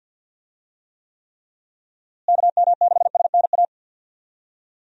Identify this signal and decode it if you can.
Morse code, heard as a keyed tone, about 51 wpm, 700 Hz. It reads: KM6SNA
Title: USA Call Signs - 6 Characters - Repeated 3x - 50wpm